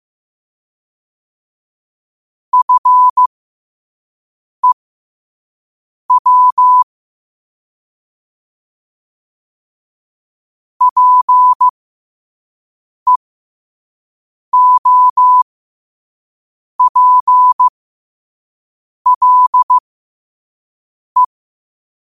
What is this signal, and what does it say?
Morse code, heard as a keyed tone, about 15 wpm, spaced as for 5 wpm, 1000 Hz. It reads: FEW PEOPLE
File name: qotdfiles / 5wpm